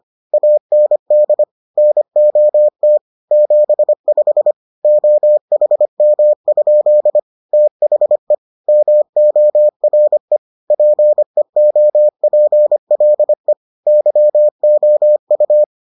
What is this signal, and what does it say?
Morse code, heard as a keyed tone, about 25 wpm, 600 Hz. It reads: AND NOT 75 OHM? THE MORE PEOPLE YOU